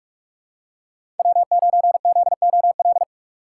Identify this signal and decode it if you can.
Morse code, heard as a keyed tone, about 46 wpm, 700 Hz. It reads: W9ZOL
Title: USA Call Signs - 6x Character Spacing - 45wpm